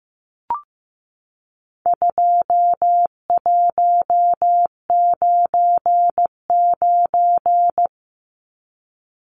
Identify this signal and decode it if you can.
Morse code, heard as a keyed tone, about 15 wpm, 700 Hz. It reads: 2199
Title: Numbers - 4-Digit 15wpm